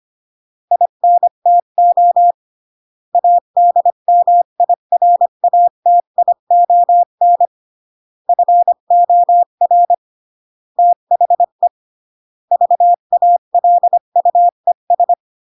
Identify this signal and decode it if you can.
Morse code, heard as a keyed tone, about 25 wpm, 700 Hz. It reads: INTO ADMIRATION FOR THE VALUES
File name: CW / jodorowsky / 82 commandments 25-20